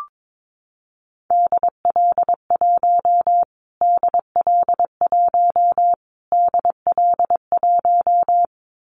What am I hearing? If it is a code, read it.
Morse code, heard as a keyed tone, about 22 wpm, 700 Hz. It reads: DL1 DL1 DL1
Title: Callsign Prefixes for Top 25 DXCC - Repeated 3x - 22wpm